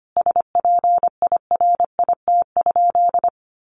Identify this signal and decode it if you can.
Morse code, heard as a keyed tone, about 25 wpm, 700 Hz. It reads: SPIRIT?
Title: Sentences from Top 1400 Words 25wpm